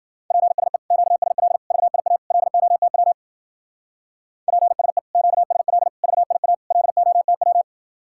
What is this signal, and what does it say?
Morse code, heard as a keyed tone, about 56 wpm, 700 Hz. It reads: PSE QSL VIA LOTW PSE QSL VIA LOTW